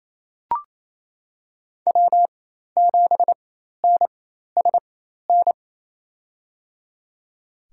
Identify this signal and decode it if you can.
Morse code, heard as a keyed tone, about 27 wpm, spaced as for 13 wpm, 700 Hz. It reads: W7NSN
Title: USA Call Signs - 5 Characters - 4x Character Spacing - 28wpm